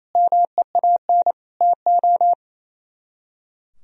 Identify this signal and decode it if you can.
Morse code, heard as a keyed tone, about 34 wpm, 700 Hz. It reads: MEAN TO
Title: Speed-Racing - Sets of 5 Words 28wpm